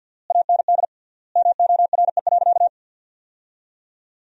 Morse code, heard as a keyed tone, about 53 wpm, 700 Hz. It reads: AND MORE.
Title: 3x Word-Spacing - Sentences from Top 500 Words - 50wpm